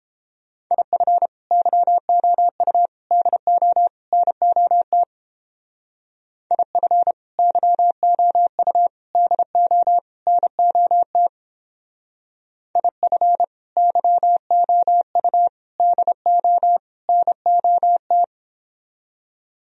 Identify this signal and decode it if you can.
Morse code, heard as a keyed tone, about 29 wpm, 700 Hz. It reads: IF YOU DO NOT IF YOU DO NOT IF YOU DO NOT